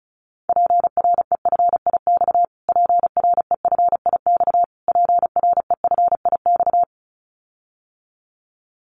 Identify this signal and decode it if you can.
Morse code, heard as a keyed tone, about 34 wpm, 700 Hz. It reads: PREFIX PREFIX PREFIX
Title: Callsign Prefixes for Top 25 DXCC - Repeated 3x - 35wpm